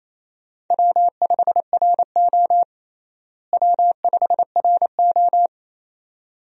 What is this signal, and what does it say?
Morse code, heard as a keyed tone, about 28 wpm, 700 Hz. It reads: W5RO W5RO
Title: USA Call Signs - 4 Characters - Repeated 2x - 28wpm